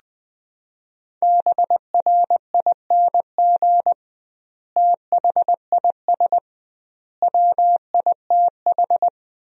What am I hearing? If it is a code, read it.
Morse code, heard as a keyed tone, about 20 wpm, 700 Hz. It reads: BRING THIS WITH
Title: 2x Word-Spacing - Sentences from Top 500 Words - 20wpm